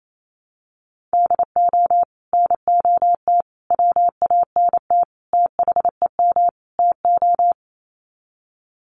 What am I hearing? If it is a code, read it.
Morse code, heard as a keyed tone, about 28 wpm, 700 Hz. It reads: DO NOT WANT THEM TO
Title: Sets of 5 Words Encore 28wpm